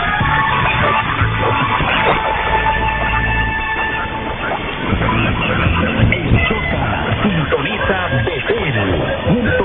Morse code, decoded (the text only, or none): none